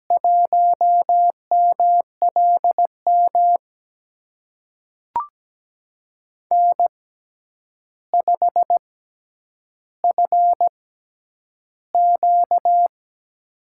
Morse code, heard as a keyed tone, about 17 wpm, spaced as for 9 wpm, 700 Hz. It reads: 1MLM N5FQ
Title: USA Call Signs - 6x Character Spacing - 17wpm